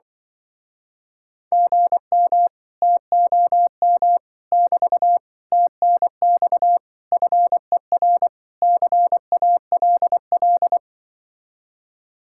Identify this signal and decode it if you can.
Morse code, heard as a keyed tone, about 24 wpm, 700 Hz. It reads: GM TOM <BT> TNX FER CALL
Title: Speed-Racing Common QSO Phrases Encore - 20wpm